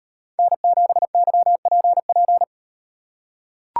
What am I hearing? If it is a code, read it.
Morse code, heard as a keyed tone, about 38 wpm, 700 Hz. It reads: N7YPP